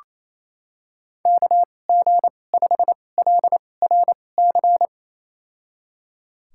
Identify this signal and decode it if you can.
Morse code, heard as a keyed tone, about 28 wpm, spaced as for 20 wpm, 700 Hz. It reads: KG5LRC
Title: USA Call Signs - 6 Characters - 2x Character Spacing - 28wpm